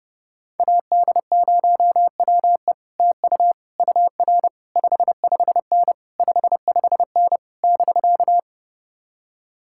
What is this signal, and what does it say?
Morse code, heard as a keyed tone, about 31 wpm, 700 Hz. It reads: AD0WE TU UR 55N 55N <BK>